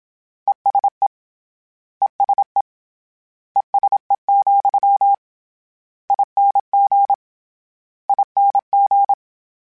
Morse code, heard as a keyed tone, about 27 wpm, 800 Hz. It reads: ESE ESE ESE, ING ING
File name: General / General-mp3s / ESE ING @25